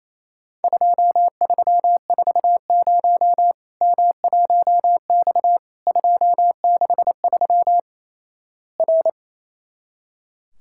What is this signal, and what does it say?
Morse code, heard as a keyed tone, about 28 wpm, 700 Hz. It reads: 2340 M1X 263 R
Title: Worked All Europe DX Contest - QTC Exchange - 28wpm